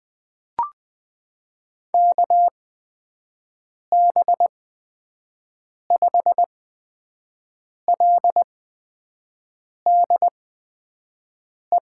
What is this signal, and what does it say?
Morse code, heard as a keyed tone, about 20 wpm, spaced as for 5 wpm, 700 Hz. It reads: KB5LDE
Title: USA Call Signs - 6 Characters - 8x Character Spacing - 20wpm